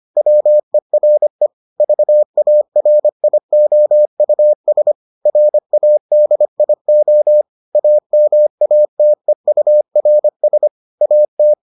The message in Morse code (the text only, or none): WERE VARIOUS RADIO AMATEURS AT